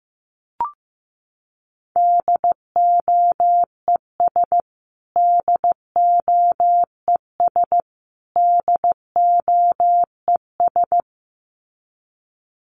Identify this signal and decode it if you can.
Morse code, heard as a keyed tone, about 15 wpm, 700 Hz. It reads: DOES DOES DOES
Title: Top 200 Words Repeated 15wpm